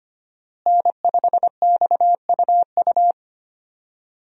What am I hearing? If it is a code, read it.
Morse code, heard as a keyed tone, about 25 wpm, 700 Hz. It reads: N5XUU
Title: USA Call Signs - 5 Characters - Repeated 2x - 25wpm